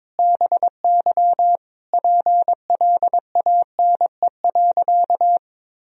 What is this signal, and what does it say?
Morse code, heard as a keyed tone, about 22 wpm, 700 Hz. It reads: BY PLANE.
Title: Sentences from Top 1000 Words - Review Part 3 - 22wpm